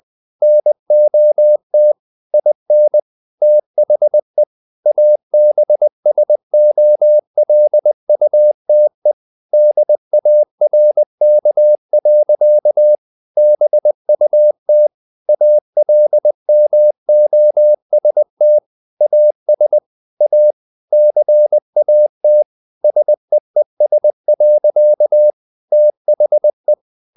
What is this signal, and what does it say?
Morse code, heard as a keyed tone, about 20 wpm, 600 Hz. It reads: NOT IN THE ABSOLUTE DARK. BUT ALMOST AS A CAT SEES. THE
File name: the old man and the sea / the old man and the sea part 1 20wpm